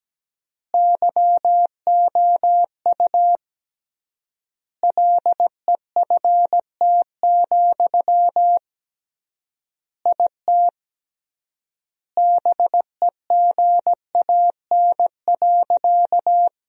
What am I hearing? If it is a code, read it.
Morse code, heard as a keyed tone, about 17 wpm, 700 Hz. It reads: YOU LEFT, IT BEGAN.
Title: 3x Word-Spacing - Sentences from Top 400 Words - 17wpm